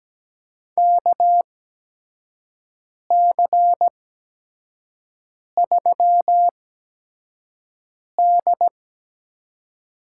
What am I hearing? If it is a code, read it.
Morse code, heard as a keyed tone, about 17 wpm, spaced as for 5 wpm, 700 Hz. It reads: KC3D
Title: USA Call Signs - 8x Character Spacing - 17wpm